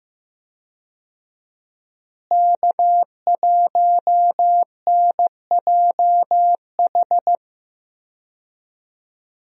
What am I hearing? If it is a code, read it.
Morse code, heard as a keyed tone, about 15 wpm, 700 Hz. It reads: K1NJH
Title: USA Call Signs - 5 Characters - Repeated 3x - 15wpm